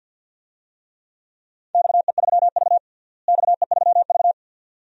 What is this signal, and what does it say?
Morse code, heard as a keyed tone, about 49 wpm, 700 Hz. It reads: XE3V XE3V
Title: DXCC-100 Call Signs - 4 Characters - Repeated 2x - 50wpm